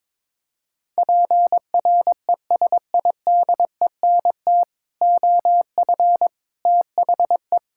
Morse code, heard as a keyed tone, about 22 wpm, 700 Hz. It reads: PRESIDENT OF THE
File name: practice / 22wpm / Sentences with 4-Word Phrases - 22wpm